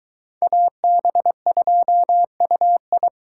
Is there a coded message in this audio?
Yes: Morse code from 0.4 to 3.1 s, heard as a keyed tone, about 21 wpm, 700 Hz. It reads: AB2UI